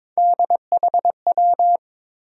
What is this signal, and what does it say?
Morse code, heard as a keyed tone, about 22 wpm, 700 Hz. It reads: DHW
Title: USA Call Sign Postfixes - 3 Letters 22wpm